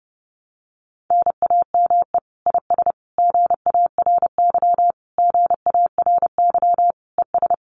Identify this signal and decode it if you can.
Morse code, heard as a keyed tone, about 30 wpm, 700 Hz. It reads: NAME IS GARY GARY ES
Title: Common QSO Phrases Encore 30wpm